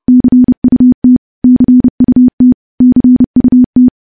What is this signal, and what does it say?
Morse code, heard as a keyed tone, about 30 wpm, 250 Hz. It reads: CUT CUT CUT
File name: morse / morse-training-30WPM